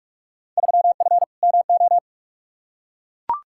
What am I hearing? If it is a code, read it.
Morse code, heard as a keyed tone, about 42 wpm, 700 Hz. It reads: 3F MO
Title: Field Day Exchanges Encore - 45wpm